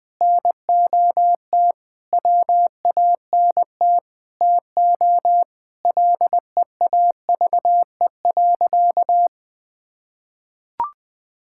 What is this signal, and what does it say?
Morse code, heard as a keyed tone, about 20 wpm, 700 Hz. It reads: NOT WANT TO LEAVE.
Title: Sentences with 5-Word Phrases - 20wpm